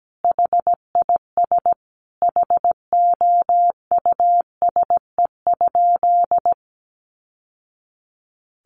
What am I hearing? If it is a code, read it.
Morse code, heard as a keyed tone, about 17 wpm, 700 Hz. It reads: HIS HOUSE?
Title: Sentences from Top 600 Words 17wpm